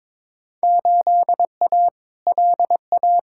Morse code, heard as a keyed tone, about 22 wpm, 700 Hz. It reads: 8A LA
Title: Field Day Exchanges Encore - 22wpm